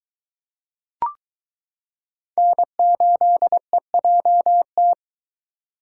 Morse code, heard as a keyed tone, about 22 wpm, 700 Hz. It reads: N8EJT